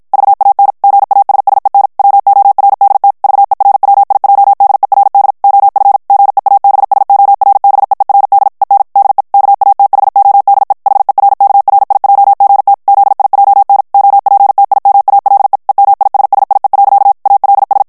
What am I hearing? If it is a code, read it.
Morse code, heard as a keyed tone, about 52 wpm, 800 Hz. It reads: VAN GASSEN WORDT VERWIJDERD OF GEABSORBEERD EN DE KATHODE HERKRIJGT ZIJN OPTIMALE EMISSIE. ALS
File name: morse mp3 / HOLLANDAIS texte clair 2 vitesse 50 mots minute